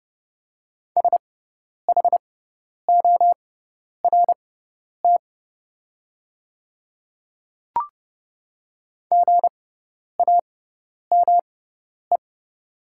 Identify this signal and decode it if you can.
Morse code, heard as a keyed tone, about 30 wpm, spaced as for 10 wpm, 690 Hz. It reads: SHORT GAME
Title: 6x Character Spacing - Top 500 Words - 30wpm